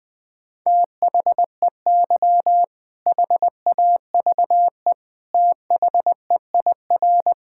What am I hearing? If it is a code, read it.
Morse code, heard as a keyed tone, about 20 wpm, 700 Hz. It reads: THEY HAVE THEIR
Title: Sentences with 2-Word Phrases - 20wpm